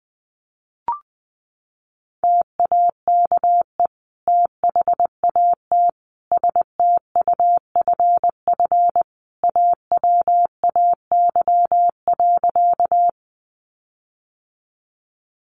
Morse code, heard as a keyed tone, about 20 wpm, 700 Hz. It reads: TAKE THAT STUFF AWAY.
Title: Sentences from Top 1400 Words 20wpm